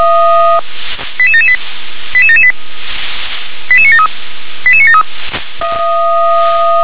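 Five-tone selective call (ZVEI), heard as quick runs of five tones, tones of about 70 ms. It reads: 88778 88778 88973 88973
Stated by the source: ZVEI five-tone call 88977 received with an SDR.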